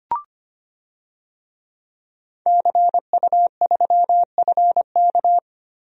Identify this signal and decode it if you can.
Morse code, heard as a keyed tone, about 25 wpm, 700 Hz. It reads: CU3FK
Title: DXCC-100 Call Signs - 5 Characters - Repeated 3x - 25wpm